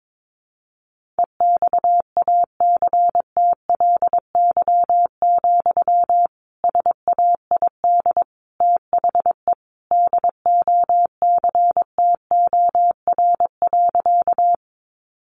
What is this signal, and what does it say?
Morse code, heard as a keyed tone, about 22 wpm, 700 Hz. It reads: EXACTLY, SAID THE DOCTOR.
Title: Sentences from Top 1500 Words 22wpm